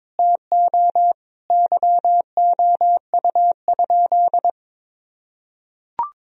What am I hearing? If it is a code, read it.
Morse code, heard as a keyed tone, about 21 wpm, 700 Hz. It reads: TO YOU?
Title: Sentences from Top 1000 Words - Review Part 2 - 22wpm